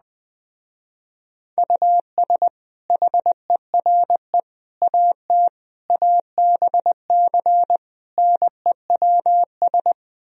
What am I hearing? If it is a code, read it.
Morse code, heard as a keyed tone, about 20 wpm, 700 Hz. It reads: US HERE AT ABC NEWS
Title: Speed-Racing - Sets of 5 Words 17wpm